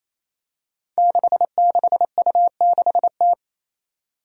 Morse code, heard as a keyed tone, about 28 wpm, 700 Hz. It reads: 66U6T
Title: Five Challenging Letter-Number Combinations 28wpm